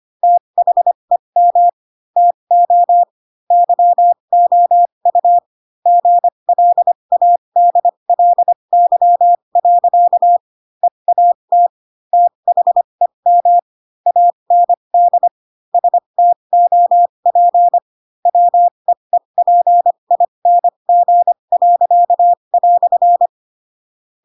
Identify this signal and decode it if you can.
Morse code, heard as a keyed tone, about 25 wpm, spaced as for 22 wpm, 700 Hz. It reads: THEM TO YOU GLADLY. EAT THEM AND STOP WEEPING."
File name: Pinocchio / Pinocchio ch7 18 25 ik1hge